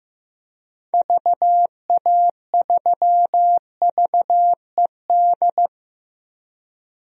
Morse code, heard as a keyed tone, about 14 wpm, 700 Hz. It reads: VA3VED
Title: DXCC-100 Call Signs - 6 Characters - 15wpm